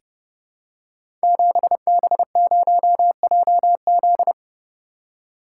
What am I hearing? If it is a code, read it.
Morse code, heard as a keyed tone, about 30 wpm, 700 Hz. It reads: 7B0JZ
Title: Code Groups - 5 Letter:Numbers - 30wpm